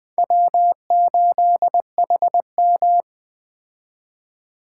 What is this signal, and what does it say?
Morse code, heard as a keyed tone, about 20 wpm, 700 Hz. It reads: W8HM
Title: Speed Racing - USA 1x2 Call Signs 17wpm